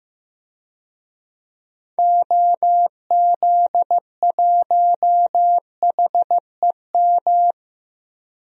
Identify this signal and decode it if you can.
Morse code, heard as a keyed tone, about 15 wpm, 700 Hz. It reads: OZ1HEM
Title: DXCC-100 Call Signs - 6 Characters - 15wpm